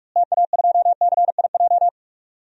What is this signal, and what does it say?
Morse code, heard as a keyed tone, about 46 wpm, 700 Hz. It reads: TA2CIJ
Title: DXCC-100 Call Signs - 6 Characters - Repeated 3x - 45wpm